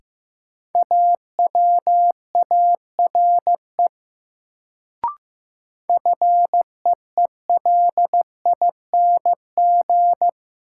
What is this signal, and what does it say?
Morse code, heard as a keyed tone, about 15 wpm, 700 Hz. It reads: AWARE FEELING
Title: Sentences from Top 1200 Words - New Word Drill Alt - 15wpm